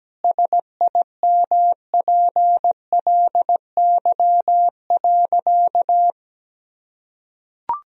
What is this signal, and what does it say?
Morse code, heard as a keyed tone, about 17 wpm, 700 Hz. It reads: SIMPLY.
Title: Sentences from Top 1000 Words - Review Part 4 - 17wpm